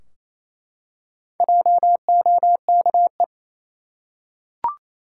Morse code, heard as a keyed tone, about 28 wpm, 700 Hz. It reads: JOKE
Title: Four Letter Words - 28wpm